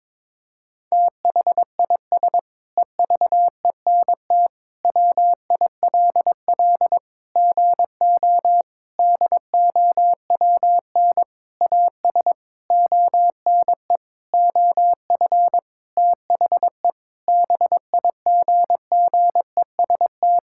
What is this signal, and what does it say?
Morse code, heard as a keyed tone, about 22 wpm, 700 Hz. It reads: THIS EVENT WILL GO DOWN AS ONE OF THE BIGGEST
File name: Sentences with 7-Word Phrases - 22wpm